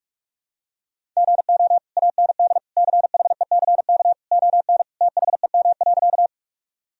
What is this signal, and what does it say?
Morse code, heard as a keyed tone, about 46 wpm, 700 Hz. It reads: GO AND CHECK ON THEM.
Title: Sentences from Top 1000 Words - Review Part 3 - 45wpm